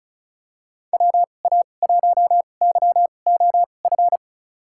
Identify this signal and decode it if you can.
Morse code, heard as a keyed tone, about 36 wpm, spaced as for 25 wpm, 700 Hz. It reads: WA1YOF
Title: USA Call Signs - 6 Characters - 2x Character Spacing - 35wpm